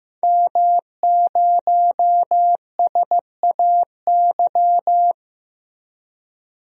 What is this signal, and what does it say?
Morse code, heard as a keyed tone, about 15 wpm, 700 Hz. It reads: M0SAY
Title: DXCC-100 Call Signs - 5 Characters - 15wpm